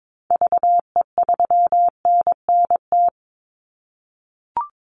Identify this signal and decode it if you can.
Morse code, heard as a keyed tone, about 22 wpm, 700 Hz. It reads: VE3NNT
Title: Call Signs - 22wpm